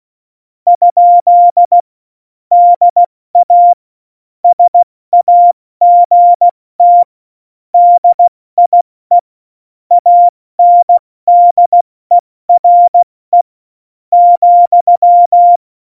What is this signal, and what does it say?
Morse code, heard as a keyed tone, about 16 wpm, 700 Hz. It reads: ? DA SAGT DIE ANDERE ,
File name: Morsekurs / Witze / Witz 22. Mai 2018, 70 F80